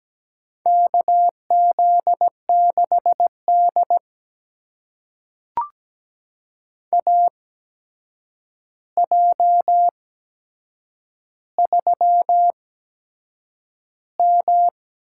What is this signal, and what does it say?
Morse code, heard as a keyed tone, about 17 wpm, spaced as for 7 wpm, 700 Hz. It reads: KZ6D AJ3M
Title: USA Call Signs - 4 Characters - 8x Character Spacing - 17wpm